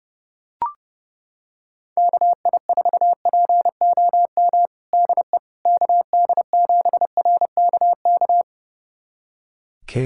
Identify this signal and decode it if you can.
Morse code, heard as a keyed tone, about 30 wpm, 700 Hz. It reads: KI4POM DE KD7RKK
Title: USA Callsign DE Callsign - 6 Characters - 30wpm